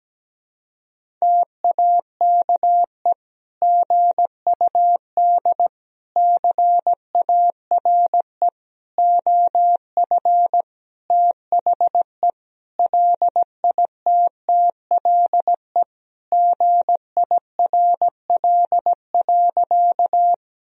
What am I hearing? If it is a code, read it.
Morse code, heard as a keyed tone, about 17 wpm, 700 Hz. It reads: TAKE GUD CARE OF THE LITTLE GIRL.
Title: Sentences from Top 400 Words and Abbreviations 17wpm